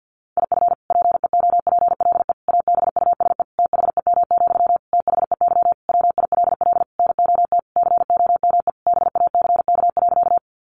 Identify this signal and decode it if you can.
Morse code, heard as a keyed tone, about 50 wpm, 700 Hz. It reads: IF PEOPLE ABUSE THEM, THEY WILL NOT COME BACK.
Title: Sentences from Top 1300 Words 50wpm